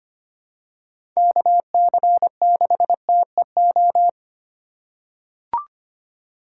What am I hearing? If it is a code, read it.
Morse code, heard as a keyed tone, about 25 wpm, 700 Hz. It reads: KC6TEO
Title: USA Call Signs - 6 Characters - 3x Character Spacing - 25wpm